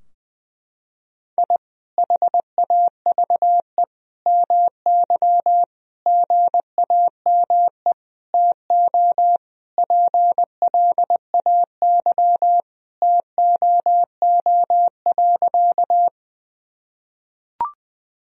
Morse code, heard as a keyed tone, about 21 wpm, 700 Hz. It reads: I HAVE MY GAME TO PLAY TOO.
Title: Sentences from Top 1000 Words - Review Part 3 - 20wpm